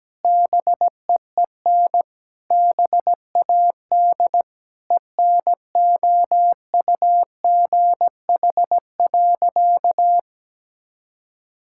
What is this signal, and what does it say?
Morse code, heard as a keyed tone, about 17 wpm, 700 Hz. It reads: BEEN BAD ENOUGH.
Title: Sentences from Top 1000 Words - Review Part 5 - 17wpm